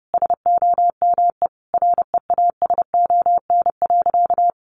Morse code, heard as a keyed tone, about 30 wpm, 700 Hz. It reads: SOME REASON.
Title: Sentences from Top 700 Words 30wpm